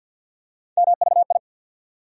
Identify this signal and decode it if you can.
Morse code, heard as a keyed tone, about 52 wpm, 700 Hz. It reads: MVI
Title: USA Call Sign Postfixes - 3 Letters 50wpm